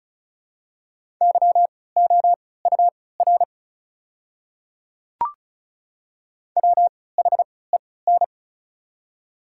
Morse code, heard as a keyed tone, about 36 wpm, spaced as for 20 wpm, 690 Hz. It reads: YOUR WHEN
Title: 3x Character Spacing - Top 100 Words - 35wpm